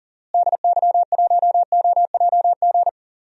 Morse code, heard as a keyed tone, about 40 wpm, 700 Hz. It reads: DY1OJG
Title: DXCC-100 Call Signs - 6 Characters - Repeated 2x - 40wpm